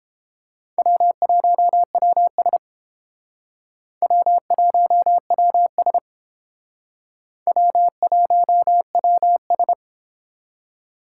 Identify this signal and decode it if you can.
Morse code, heard as a keyed tone, about 30 wpm, 700 Hz. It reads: W1WS W1WS W1WS